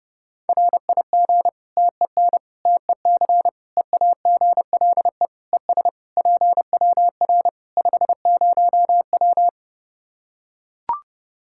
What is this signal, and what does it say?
Morse code, heard as a keyed tone, about 30 wpm, 700 Hz. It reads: RIG TEN TEC EAGLE ES PWR 50W
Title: Practice QSO - Exchange 2 - Info - 30wpm